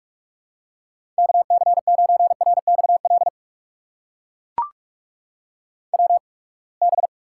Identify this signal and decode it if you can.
Morse code, heard as a keyed tone, about 41 wpm, spaced as for 30 wpm, 700 Hz. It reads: KC9RXL WB
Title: USA Call Signs - 6 Characters - 8x Character Spacing - 45wpm